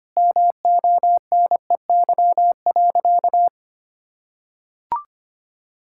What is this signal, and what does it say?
Morse code, heard as a keyed tone, about 25 wpm, 700 Hz. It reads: MONEY.
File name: practice / 25wpm / Sentences with 4-Word Phrases - 25wpm